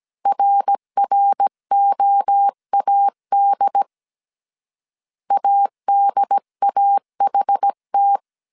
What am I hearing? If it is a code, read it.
Morse code, heard as a keyed tone, about 17 wpm, 800 Hz. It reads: RROAD ADAHT